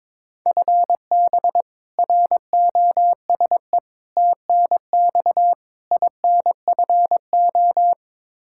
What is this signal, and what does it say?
Morse code, heard as a keyed tone, about 21 wpm, 700 Hz. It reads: FB ROSE TNX INFO
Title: Practice QSO - Exchange 3 - More Info - 22wpm